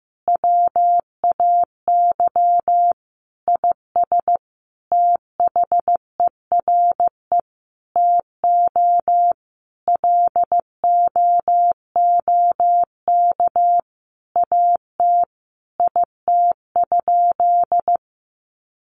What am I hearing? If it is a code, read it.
Morse code, heard as a keyed tone, about 15 wpm, 700 Hz. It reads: WAY IS THERE TO LOOK AT IT?
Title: Sentences from Top 100 Words 15wpm